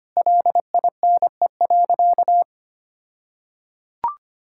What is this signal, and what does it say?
Morse code, heard as a keyed tone, about 25 wpm, 700 Hz. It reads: LINE.
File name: practice / 25wpm / All-Age Appropriate Jokes - 25wpm